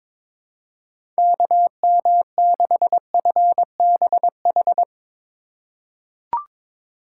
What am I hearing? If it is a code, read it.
Morse code, heard as a keyed tone, about 22 wpm, 700 Hz. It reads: KM6FBH
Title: USA Call Signs - 6 Characters - 4x Character Spacing - 22wpm